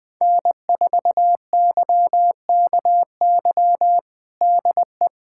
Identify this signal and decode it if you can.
Morse code, heard as a keyed tone, about 20 wpm, 700 Hz. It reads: N4YKY DE
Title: USA Callsign DE Callsign 20wpm